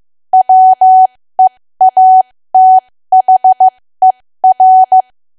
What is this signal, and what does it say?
Morse code, heard as a keyed tone, about 15 wpm, spaced as for 13 wpm, 750 Hz. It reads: WEATHER
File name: cwa / W1AW-Level2 / 13WPM